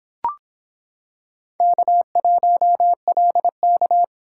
Morse code, heard as a keyed tone, about 26 wpm, 700 Hz. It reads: K1LK